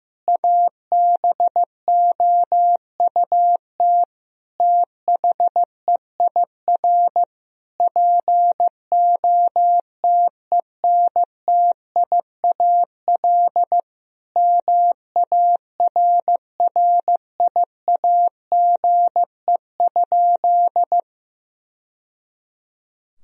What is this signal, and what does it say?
Morse code, heard as a keyed tone, about 16 wpm, 700 Hz. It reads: ABOUT THEIR POTENTIAL MARRIAGE?
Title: Sentences from Top 1000 Words - Review Part 4 - 15wpm